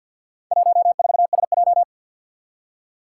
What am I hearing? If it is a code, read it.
Morse code, heard as a keyed tone, about 50 wpm, 700 Hz. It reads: 14SJ
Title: Four Challenging Letter-Number Combinations 50wpm